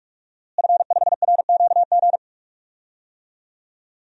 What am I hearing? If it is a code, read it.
Morse code, heard as a keyed tone, about 46 wpm, 700 Hz. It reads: F5RQG
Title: DXCC-100 Call Signs - 5 Characters - Repeated 3x - 45wpm